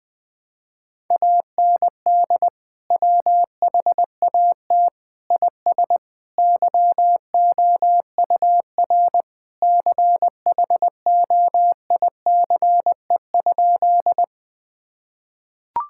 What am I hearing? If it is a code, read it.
Morse code, heard as a keyed tone, about 21 wpm, 700 Hz. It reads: AND WHAT IS YOUR CHOICE?
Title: Sentences from Top 1000 Words - Review Part 6 - 20wpm